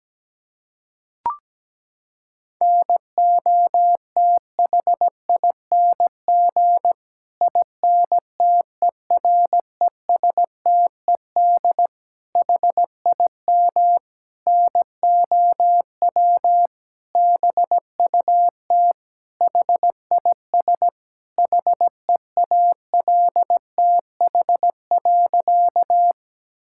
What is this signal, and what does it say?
Morse code, heard as a keyed tone, about 17 wpm, 700 Hz. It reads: NOTHING INTERESTED HIM NOW BUT HIS HEALTH.
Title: Sentences from Top 1000 Words - Review Part 6 - 17wpm